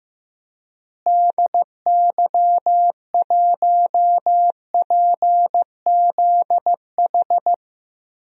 Morse code, heard as a keyed tone, about 15 wpm, 700 Hz. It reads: DY1PZH
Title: DXCC-100 Call Signs - 6 Characters - Repeated 3x - 15wpm